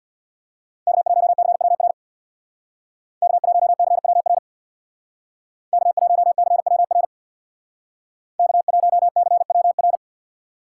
Morse code, heard as a keyed tone, about 64 wpm, 700 Hz. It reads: K1CWR K1CWR K1CWR K1CWR